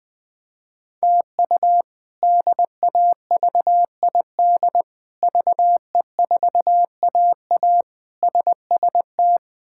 Morse code, heard as a keyed tone, about 21 wpm, 700 Hz. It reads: TU DAVID VE4AA SST
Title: K1USN SST QSO Practice - 20wpm